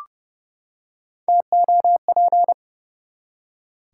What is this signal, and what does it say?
Morse code, heard as a keyed tone, about 30 wpm, 700 Hz. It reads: TOP
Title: Three Letter Words - 30wpm